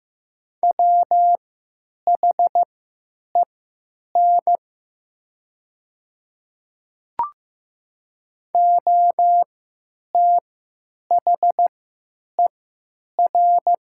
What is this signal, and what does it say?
Morse code, heard as a keyed tone, about 15 wpm, spaced as for 8 wpm, 700 Hz. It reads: WHEN OTHER
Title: 3x Character Spacing - Top 300 Words - 15wpm